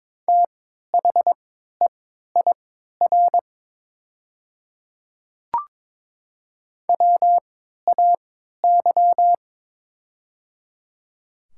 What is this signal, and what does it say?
Morse code, heard as a keyed tone, about 23 wpm, spaced as for 13 wpm, 700 Hz. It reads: THEIR WAY
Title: 3x Character Spacing - Top 100 Words - 22wpm